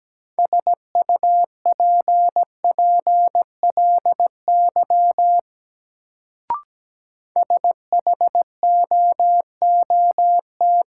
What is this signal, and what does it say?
Morse code, heard as a keyed tone, about 17 wpm, 700 Hz. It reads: SUPPLY SHOOT
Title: Sentences from Top 1200 Words - New Word Drill Alt - 17wpm